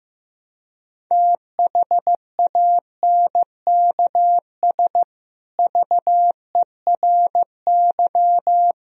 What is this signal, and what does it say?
Morse code, heard as a keyed tone, about 16 wpm, 700 Hz. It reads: THANKS VERY
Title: Sentences from Top 1000 Words - Review Part 3 - 15wpm